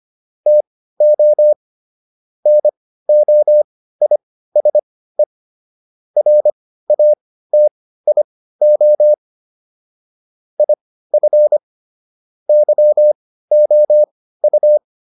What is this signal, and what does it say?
Morse code, heard as a keyed tone, about 25 wpm, spaced as for 15 wpm, 600 Hz. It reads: TO NOISE RATIO IF YOU